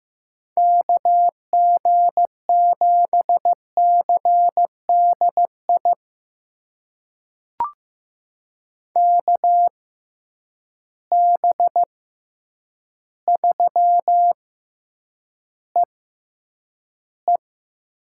Morse code, heard as a keyed tone, about 15 wpm, spaced as for 8 wpm, 700 Hz. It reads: KG7CDI KB3EE
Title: USA Call Signs - 6 Characters - 6x Character Spacing - 15wpm